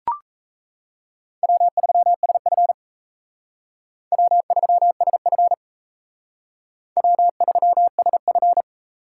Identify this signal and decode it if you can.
Morse code, heard as a keyed tone, about 35 wpm, 700 Hz. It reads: W3SF W3SF W3SF
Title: Speed Racing - USA 1x2 Call Signs 28wpm